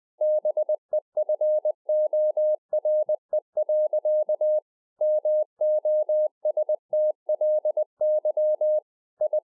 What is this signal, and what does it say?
Morse code, heard as a keyed tone, about 20 wpm, 610 Hz. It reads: BEFORE. MOSTLY I